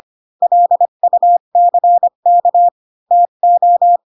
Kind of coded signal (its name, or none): Morse code